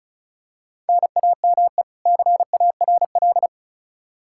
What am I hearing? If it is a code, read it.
Morse code, heard as a keyed tone, about 36 wpm, 700 Hz. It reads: NAME CARL
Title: Practice QSO - Exchange 1 - Beginning - 35wpm